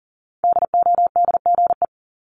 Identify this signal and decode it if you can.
Morse code, heard as a keyed tone, about 41 wpm, 700 Hz. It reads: DODGE
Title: Multi-letter Phonemes 40wpm